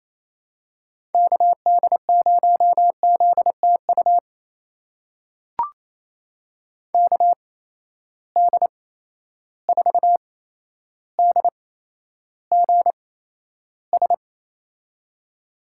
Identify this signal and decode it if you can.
Morse code, heard as a keyed tone, about 28 wpm, spaced as for 12 wpm, 700 Hz. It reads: KD0ZTU KD4DGS
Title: USA Call Signs - 6 Characters - 8x Character Spacing - 28wpm